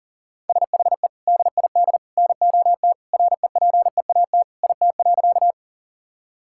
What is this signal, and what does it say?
Morse code, heard as a keyed tone, about 40 wpm, 700 Hz. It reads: SHE DID NOT REPEAT IT.
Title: Sentences from Top 1600 Words 40wpm